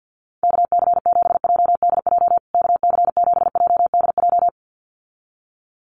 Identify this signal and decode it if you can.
Morse code, heard as a keyed tone, about 48 wpm, 700 Hz. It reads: KC7JDJ KC7JDJ
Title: USA Call Signs Repeated 2x 50wpm